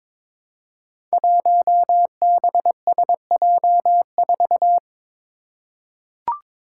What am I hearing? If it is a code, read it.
Morse code, heard as a keyed tone, about 22 wpm, 700 Hz. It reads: 1BSJ4
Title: Five Challenging Letter-Number Combinations 22wpm